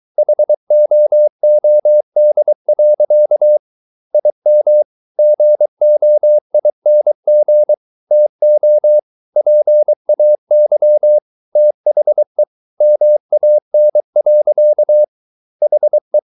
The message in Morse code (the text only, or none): HOOD. IM GOING TO PAY THE MAN. HE